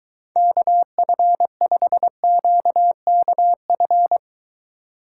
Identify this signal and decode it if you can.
Morse code, heard as a keyed tone, about 22 wpm, 700 Hz. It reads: KF5QKF